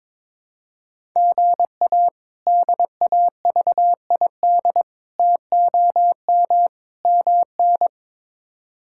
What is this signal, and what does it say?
Morse code, heard as a keyed tone, about 22 wpm, 700 Hz. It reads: GA DAVID TOM MN
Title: K1USN SST QSO Practice - 22wpm